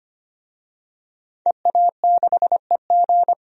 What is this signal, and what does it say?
Morse code, heard as a keyed tone, about 25 wpm, 700 Hz. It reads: EA6EG
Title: DXCC-100 Call Signs - 5 Characters - 25wpm